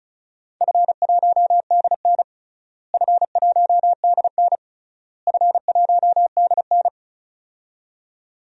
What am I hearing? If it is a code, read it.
Morse code, heard as a keyed tone, about 36 wpm, 700 Hz. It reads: F1DN F1DN F1DN
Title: DXCC-100 Call Signs - 4 Characters - Repeated 3x - 35wpm